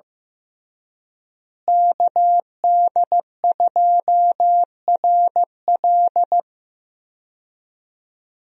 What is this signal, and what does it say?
Morse code, heard as a keyed tone, about 15 wpm, 700 Hz. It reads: KD2RL
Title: USA Call Signs - 5 Characters - Repeated 2x - 15wpm